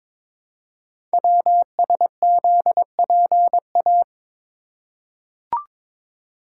Morse code, heard as a keyed tone, about 23 wpm, 700 Hz. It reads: WSZPA
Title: Code Groups - 5 Letters - 22wpm